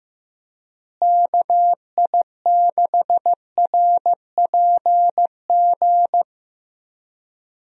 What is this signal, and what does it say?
Morse code, heard as a keyed tone, about 15 wpm, 700 Hz. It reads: KI6RPG
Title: USA Call Signs - 6 Characters - Repeated 2x - 15wpm